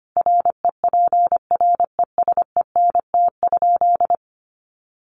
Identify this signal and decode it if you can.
Morse code, heard as a keyed tone, about 25 wpm, 700 Hz. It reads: REPRESENT?
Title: Sentences from Top 1200 Words 25wpm